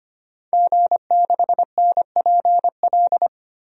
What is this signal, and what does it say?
Morse code, heard as a keyed tone, about 25 wpm, 700 Hz. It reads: G6NPL
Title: DXCC-100 Call Signs - 5 Characters - 25wpm